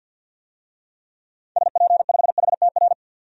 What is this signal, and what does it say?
Morse code, heard as a keyed tone, about 50 wpm, 700 Hz. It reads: SP5HTR
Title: DXCC-100 Call Signs - 6 Characters - 50wpm